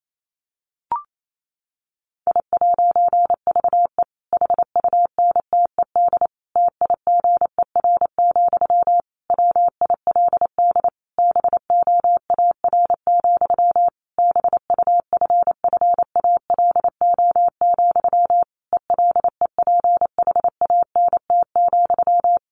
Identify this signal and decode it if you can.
Morse code, heard as a keyed tone, about 28 wpm, 700 Hz. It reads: I'VE HUNTED TIGER, WILD BOAR, BUFFALO, ELEPHANT,
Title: A Sound Of Thunder 28wpm